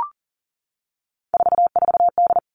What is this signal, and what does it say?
Morse code, heard as a keyed tone, about 40 wpm, 700 Hz. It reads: 44D
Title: All Cut Numbers - 3-Digit 40wpm